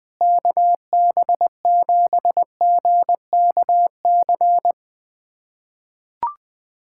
Morse code, heard as a keyed tone, about 20 wpm, 700 Hz. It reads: KB7GKC
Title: USA Callsign DE Callsign - 6 Characters - 20wpm